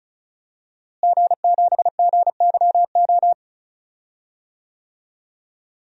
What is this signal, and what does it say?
Morse code, heard as a keyed tone, about 34 wpm, 700 Hz. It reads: G7GYO
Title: DXCC-100 Call Signs - 5 Characters - Repeated 2x - 35wpm